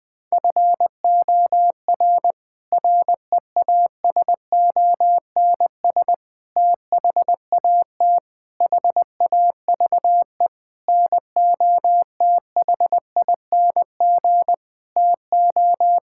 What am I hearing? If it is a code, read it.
Morse code, heard as a keyed tone, about 20 wpm, 700 Hz. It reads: FOR REASONS THAT HAVE NOTHING TO
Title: Sets of 8 Words - 20wpm